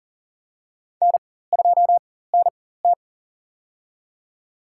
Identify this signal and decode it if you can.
Morse code, heard as a keyed tone, about 40 wpm, spaced as for 19 wpm, 700 Hz. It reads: N2NT
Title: USA Call Signs - 4 Characters - 4x Character Spacing - 40wpm